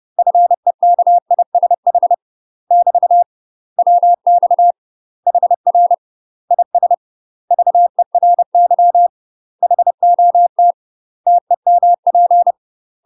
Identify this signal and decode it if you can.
Morse code, heard as a keyed tone, about 30 wpm, 700 Hz. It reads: FEKISH <BT> WX HR IS VERY HOT TEMP
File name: UK MASTER QSOs TRAINING / 30WPM / UK MASTER QSOs 0050